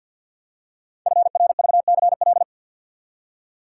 Morse code, heard as a keyed tone, about 52 wpm, 700 Hz. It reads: FRVCL
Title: Code Groups - 5 Letters - 50wpm